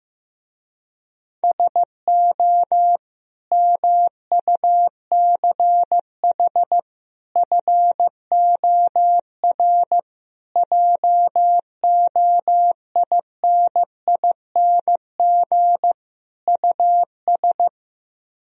Morse code, heard as a keyed tone, about 15 wpm, 700 Hz. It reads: SO MUCH FOR JOINING US
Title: Speed-Racing - Sets of 5 Words 15wpm